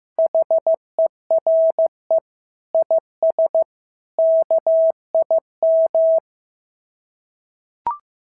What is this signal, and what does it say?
Morse code, heard as a keyed tone, about 15 wpm, 650 Hz. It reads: HERE IS KIM
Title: SOTA QSO Practice 15wpm